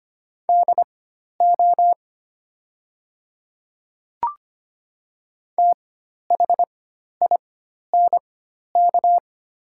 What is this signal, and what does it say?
Morse code, heard as a keyed tone, about 25 wpm, spaced as for 12 wpm, 690 Hz. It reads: DO THINK
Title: 4x Character Spacing - Top 200 Words - 25wpm